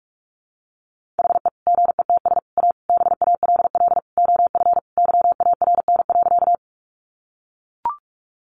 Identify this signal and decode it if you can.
Morse code, heard as a keyed tone, about 44 wpm, 700 Hz. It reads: HE GETS A BALL OF YARN.
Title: Additional Easy Phrases 45wpm